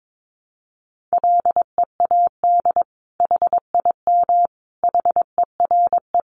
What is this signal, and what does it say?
Morse code, heard as a keyed tone, about 22 wpm, 700 Hz. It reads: LEAD HIM HERE
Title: Sentences from Top 1200 Words 22wpm